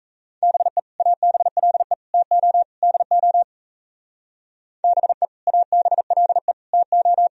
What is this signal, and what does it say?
Morse code, heard as a keyed tone, about 36 wpm, 700 Hz. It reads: BE ABLE TO DO BE ABLE TO